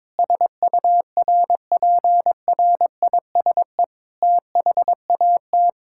Morse code, heard as a keyed tone, about 22 wpm, 700 Hz. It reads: SURPRISE THAT
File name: Sets of 9 Words - 22wpm